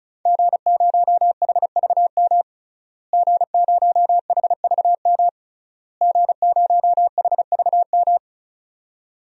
Morse code, heard as a keyed tone, about 35 wpm, 700 Hz. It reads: G0HVM G0HVM G0HVM
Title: DXCC-100 Call Signs - 5 Characters - Repeated 3x - 35wpm